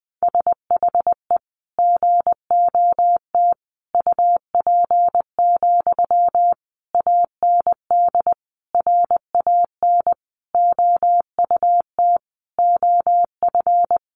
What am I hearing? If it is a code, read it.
Morse code, heard as a keyed tone, about 19 wpm, 700 Hz. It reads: SHE GOT UP, AND RAN OUT OF
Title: Sentences from Top 500 Words 20wpm